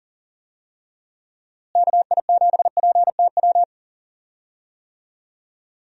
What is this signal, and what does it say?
Morse code, heard as a keyed tone, about 39 wpm, 700 Hz. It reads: KI7PTW
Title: USA Call Signs - 6 Characters - Repeated 2x - 40wpm